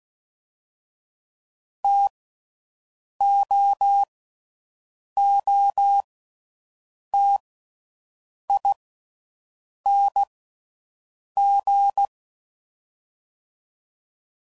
Morse code, heard as a keyed tone, about 16 wpm, 800 Hz. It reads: T O O T I N G
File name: Mixed / Variable Sp / T AOEIs